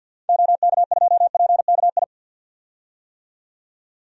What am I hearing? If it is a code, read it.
Morse code, heard as a keyed tone, about 53 wpm, 700 Hz. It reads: OK2PKI
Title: DXCC-100 Call Signs - 6 Characters - 50wpm